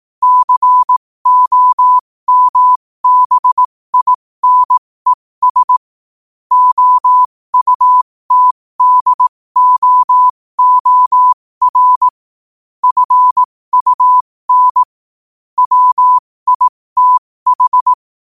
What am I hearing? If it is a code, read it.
Morse code, heard as a keyed tone, about 18 wpm, spaced as for 15 wpm, 1000 Hz. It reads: COMBINES OUTDOOR FUN WITH
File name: qotdfiles / 15wpm